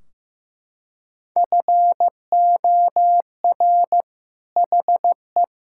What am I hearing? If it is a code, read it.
Morse code, heard as a keyed tone, about 15 wpm, 700 Hz. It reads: FOR HE
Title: Sentences from Top 1600 Words 15wpm